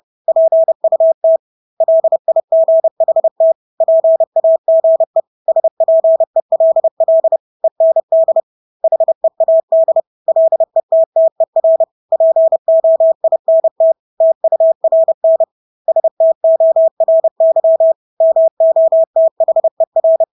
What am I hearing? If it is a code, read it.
Morse code, heard as a keyed tone, about 30 wpm, 650 Hz. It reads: PUT LIGHT PAGE SPELL END HEAD LETTER POINT TURN STORY MOTHER
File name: Engelske ord 30-30 wpm